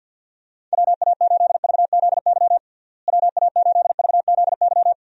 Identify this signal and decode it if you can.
Morse code, heard as a keyed tone, about 51 wpm, 700 Hz. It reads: WA8VZY WA8VZY
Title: USA Call Signs - 6 Characters - Repeated 3x - 50wpm